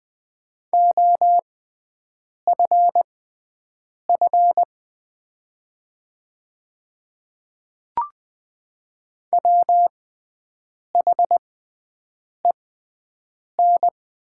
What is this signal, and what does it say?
Morse code, heard as a keyed tone, about 20 wpm, spaced as for 7 wpm, 700 Hz. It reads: OFF WHEN
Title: 6x Character Spacing - Top 300 Words - 20wpm